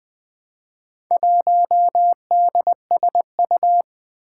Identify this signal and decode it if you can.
Morse code, heard as a keyed tone, about 20 wpm, 700 Hz. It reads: 1DSU
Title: Four Challenging Letter-Number Combinations 20wpm